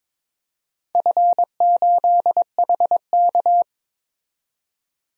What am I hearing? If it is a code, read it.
Morse code, heard as a keyed tone, about 21 wpm, 700 Hz. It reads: F8HK
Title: DXCC-100 Call Signs - 4 Characters - 22wpm